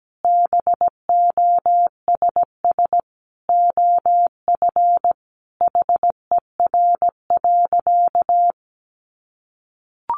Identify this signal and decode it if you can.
Morse code, heard as a keyed tone, about 17 wpm, 700 Hz. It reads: BOSS OF HER.
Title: Sentences from Top 700 Words 17wpm